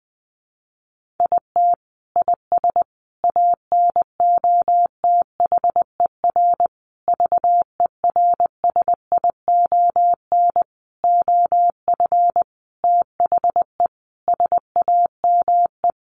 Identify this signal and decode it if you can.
Morse code, heard as a keyed tone, about 20 wpm, 700 Hz. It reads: IT IS ANOTHER VERSION OF THE SAME
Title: Sentences from Top 1400 Words 20wpm